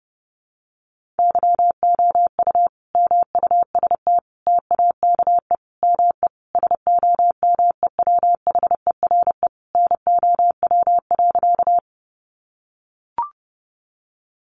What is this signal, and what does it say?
Morse code, heard as a keyed tone, about 30 wpm, 700 Hz. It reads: YOU MUST TAKE ME SOMEWHERE NOW.
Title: Sentences from Top 1400 Words 30wpm